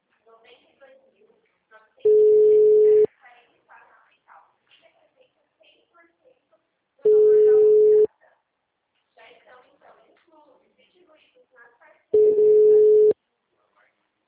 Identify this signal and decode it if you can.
Ringback tone (Europe, 425 Hz), with speech.
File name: out- -2030